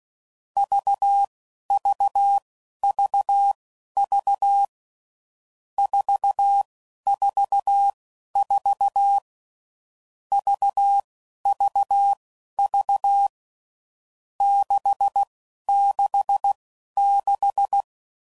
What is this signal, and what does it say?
Morse code, heard as a keyed tone, about 16 wpm, spaced as for 11 wpm, 800 Hz. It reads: VVVV 444 VVV 666
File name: Groups / Basic Letter Groups / VB46